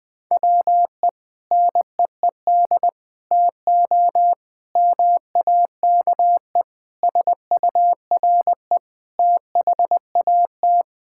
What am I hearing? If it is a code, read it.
Morse code, heard as a keyed tone, about 20 wpm, 700 Hz. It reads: WE NEED TO MAKE SURE THAT
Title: Sentences with 5-Word Phrases - 20wpm